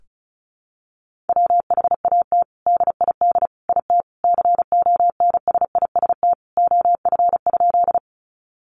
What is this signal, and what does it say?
Morse code, heard as a keyed tone, about 34 wpm, 700 Hz. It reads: WHAT DID IT CONSIST OF?
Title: Sentences from Top 1400 Words 35wpm